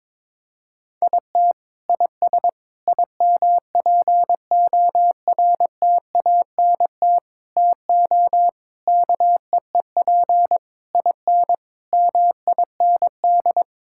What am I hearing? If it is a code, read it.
Morse code, heard as a keyed tone, about 22 wpm, 700 Hz. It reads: IT IS IMPORTANT TO KEEP IN MIND
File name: practice / 22wpm / Sets of 8 Words - 22wpm